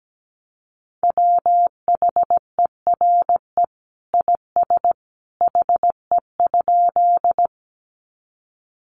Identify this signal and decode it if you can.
Morse code, heard as a keyed tone, about 17 wpm, 700 Hz. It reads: WHERE IS HE?
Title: Sentences from Top 200 Words 17wpm